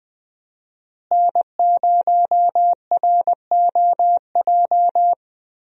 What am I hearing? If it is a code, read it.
Morse code, heard as a keyed tone, about 20 wpm, 700 Hz. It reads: N0ROJ
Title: USA Call Signs - 5 Characters - Repeated 3x - 20wpm